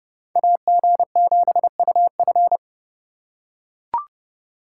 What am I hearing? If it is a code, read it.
Morse code, heard as a keyed tone, about 30 wpm, 700 Hz. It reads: AG7UF